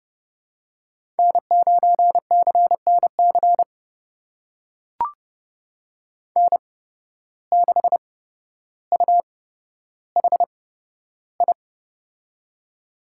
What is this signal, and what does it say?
Morse code, heard as a keyed tone, about 30 wpm, spaced as for 13 wpm, 700 Hz. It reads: N9CNC N6UHI
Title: USA Call Signs - 5 Characters - 8x Character Spacing - 30wpm